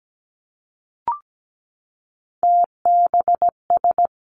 Morse code, heard as a keyed tone, about 17 wpm, 700 Hz. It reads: TBS
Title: Three Challenging Letter-Number Combinations 17wpm